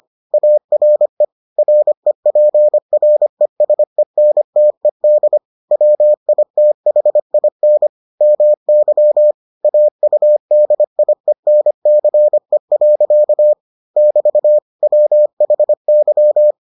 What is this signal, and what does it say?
Morse code, heard as a keyed tone, about 25 wpm, 600 Hz. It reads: ARE REPRESENTED WITHIN MY AUDIENCE. <BT> WHY